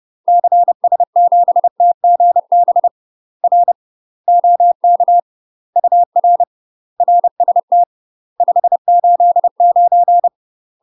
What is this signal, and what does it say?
Morse code, heard as a keyed tone, about 30 wpm, 700 Hz. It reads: CS7TGB R OK UR RST 589
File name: UK MASTER QSOs TRAINING / 30WPM / UK MASTER QSOs 0049